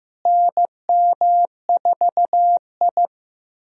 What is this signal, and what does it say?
Morse code, heard as a keyed tone, about 15 wpm, 700 Hz. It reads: NM4I
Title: USA Call Signs - 4 Characters - Repeated 3x - 15wpm